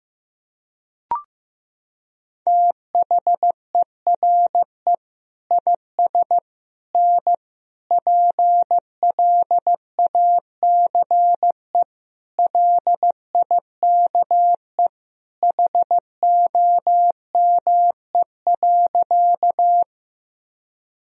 Morse code, heard as a keyed tone, about 15 wpm, 700 Hz. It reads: THERE IS N PLACE LIKE HOME.
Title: Sentences from Top 200 Words and Abbreviations 15wpm